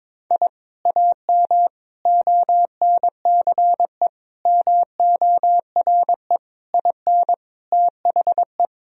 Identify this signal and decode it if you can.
Morse code, heard as a keyed tone, about 22 wpm, 700 Hz. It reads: I AM ONCE MORE IN THE
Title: Sentences from Top 1000 Words - Review Part 3 - 22wpm